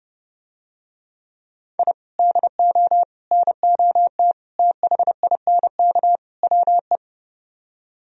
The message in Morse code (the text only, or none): I DO NOT THINK WE